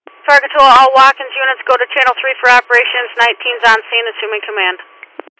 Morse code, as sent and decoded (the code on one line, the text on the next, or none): none